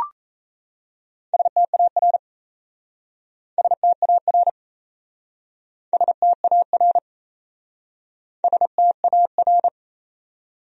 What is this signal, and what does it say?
Morse code, heard as a keyed tone, about 37 wpm, 700 Hz. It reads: STAR STAR STAR STAR